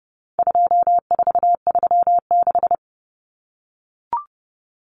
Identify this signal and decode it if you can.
Morse code, heard as a keyed tone, about 30 wpm, 700 Hz. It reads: 2436
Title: Numbers - 4-Digit 30wpm